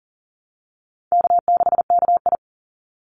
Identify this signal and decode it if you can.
Morse code, heard as a keyed tone, about 40 wpm, 700 Hz. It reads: K6KI
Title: Easy Call Signs - 40wpm